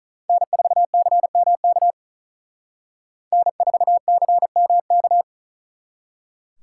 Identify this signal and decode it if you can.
Morse code, heard as a keyed tone, about 46 wpm, 700 Hz. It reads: N4CMK N4CMK